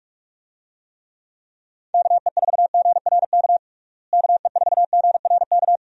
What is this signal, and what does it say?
Morse code, heard as a keyed tone, about 46 wpm, 700 Hz. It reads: KE4GRK KE4GRK
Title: USA Call Signs - 6 Characters - Repeated 2x - 45wpm